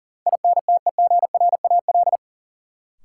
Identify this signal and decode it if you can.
Morse code, heard as a keyed tone, about 40 wpm, 700 Hz. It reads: INTEGRAL
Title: Sets of 5 Words - New Word Drill - 40wpm